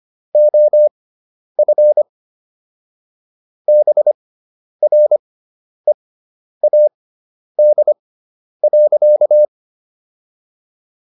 Morse code, heard as a keyed tone, about 25 wpm, spaced as for 10 wpm, 600 Hz. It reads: OF BREAD.